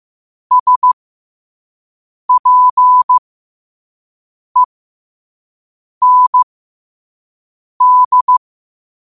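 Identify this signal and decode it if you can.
Morse code, heard as a keyed tone, about 15 wpm, spaced as for 5 wpm, 1000 Hz. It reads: SPEND
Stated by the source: Morse quote of the day at 5 wpm.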